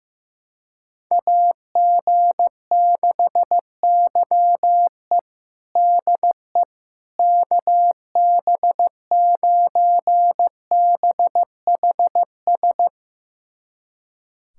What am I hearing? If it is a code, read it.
Morse code, heard as a keyed tone, about 15 wpm, 700 Hz. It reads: AG6YE DE KB9BHS
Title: USA Callsign DE Callsign 15wpm